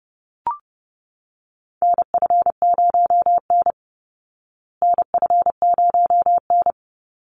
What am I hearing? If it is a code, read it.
Morse code, heard as a keyed tone, about 30 wpm, 700 Hz. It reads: NF0N NF0N
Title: Call Signs Repeated 3x - 30wpm